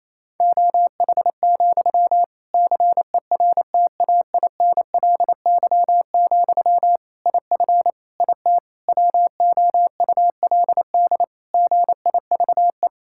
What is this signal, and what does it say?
Morse code, heard as a keyed tone, about 28 wpm, 700 Hz. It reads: OH, CERTAINLY, IF IT WOULD GIVE
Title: Sentences from Top 1600 Words 28wpm